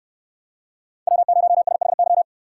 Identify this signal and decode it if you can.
Morse code, heard as a keyed tone, about 68 wpm, 690 Hz. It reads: W0SHY